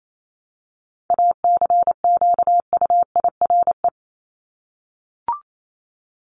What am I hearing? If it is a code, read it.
Morse code, heard as a keyed tone, about 28 wpm, 700 Hz. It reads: ACQUIRE
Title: Top 1501-1750 Words 28wpm